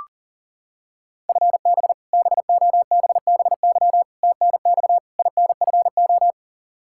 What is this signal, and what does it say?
Morse code, heard as a keyed tone, about 40 wpm, 700 Hz. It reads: FB BOBBY TNX INFO
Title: Practice QSO - Exchange 3 - More Info - 40wpm